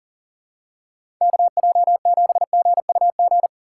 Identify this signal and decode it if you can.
Morse code, heard as a keyed tone, about 37 wpm, 700 Hz. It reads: KJ7GUG
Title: USA Call Signs - 6x Character Spacing - 40wpm